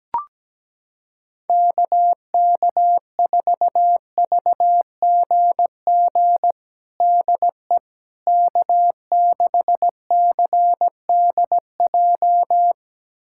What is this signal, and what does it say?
Morse code, heard as a keyed tone, about 17 wpm, 700 Hz. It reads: KK4VGG DE K6CDJ
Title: USA Callsign DE Callsign 17wpm